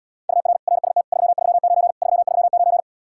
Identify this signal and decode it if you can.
Morse code, heard as a keyed tone, about 80 wpm, 700 Hz. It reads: UR RST 339 339